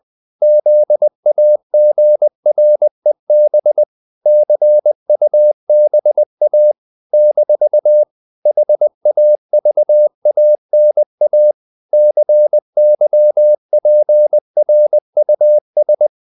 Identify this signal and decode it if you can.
Morse code, heard as a keyed tone, about 20 wpm, 600 Hz. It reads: ZAGREB CUBA - HAVANA CYPRUS
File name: CW / Practice / NationalCapitals20